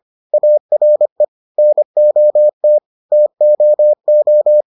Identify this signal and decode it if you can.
Morse code, heard as a keyed tone, about 25 wpm, 600 Hz. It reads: ARE NOT TOO